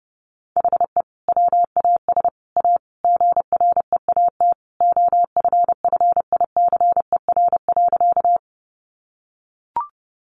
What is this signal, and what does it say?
Morse code, heard as a keyed tone, about 30 wpm, 700 Hz. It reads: HE WAS A GREAT OFFICER.
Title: Sentences from Top 1200 Words 30wpm